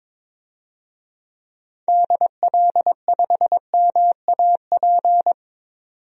Morse code, heard as a keyed tone, about 22 wpm, 700 Hz. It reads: DL5MAP
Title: DXCC-100 Call Signs - 6 Characters - 22wpm